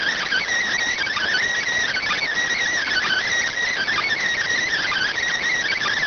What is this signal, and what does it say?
Emergency Alert System (SAME) tones (header bursts), heard as harsh buzzing bursts. It is heard from 0.9 to 5.1 s.